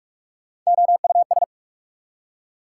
Morse code, heard as a keyed tone, about 59 wpm, spaced as for 52 wpm, 690 Hz. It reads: OUS
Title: Speed-Racing Prefixes and Suffixes 45wpm